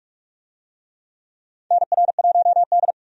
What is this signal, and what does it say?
Morse code, heard as a keyed tone, about 46 wpm, 700 Hz. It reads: NR1D
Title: USA Call Signs - 4 Characters - 45wpm